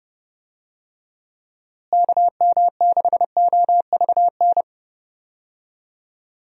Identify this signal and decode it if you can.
Morse code, heard as a keyed tone, about 30 wpm, 700 Hz. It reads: KM6OVN
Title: USA Call Signs - 6 Characters - Repeated 2x - 30wpm